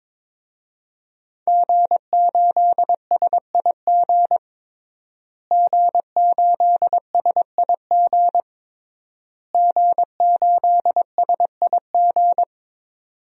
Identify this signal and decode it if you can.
Morse code, heard as a keyed tone, about 22 wpm, 700 Hz. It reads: G8SIG G8SIG G8SIG
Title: DXCC-100 Call Signs - 5 Characters - Repeated 3x - 22wpm